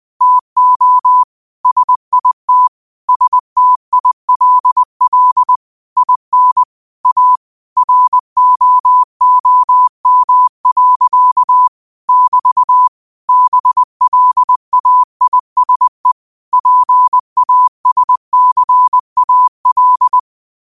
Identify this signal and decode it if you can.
Morse code, heard as a keyed tone, about 20 wpm, 1000 Hz. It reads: TO SIT STILL IN A ROOM. <BT> BLAISE PASCAL
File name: qotdfiles / 20wpm